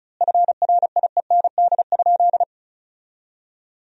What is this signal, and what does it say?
Morse code, heard as a keyed tone, about 34 wpm, 700 Hz. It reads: FRIEND?
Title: 4x Word-Spacing - Sentences from Top 300 Words - 35wpm